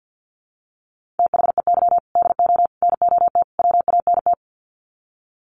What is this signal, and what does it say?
Morse code, heard as a keyed tone, about 48 wpm, 700 Hz. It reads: THEY DO NOT WANT
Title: Sets of 4 Words 50wpm